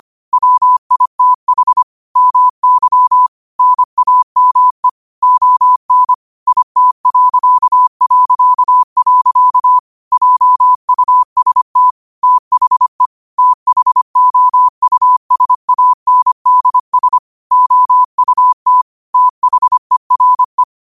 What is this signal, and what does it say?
Morse code, heard as a keyed tone, about 25 wpm, 1000 Hz. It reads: WITH MY NAME ON IT... JUST THE THOUSANDS OUT THERE